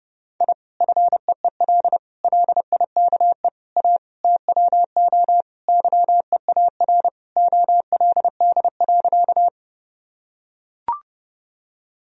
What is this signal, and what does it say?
Morse code, heard as a keyed tone, about 30 wpm, 700 Hz. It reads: I FEEL LIKE A TWO YEAR OLD.
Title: Sentences from Top 1000 Words - Review Part 1 - 30wpm